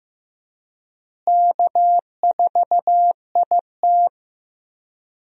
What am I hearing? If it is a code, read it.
Morse code, heard as a keyed tone, about 15 wpm, 700 Hz. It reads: K4IT
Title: USA Call Signs - 4 Characters - Repeated 3x - 15wpm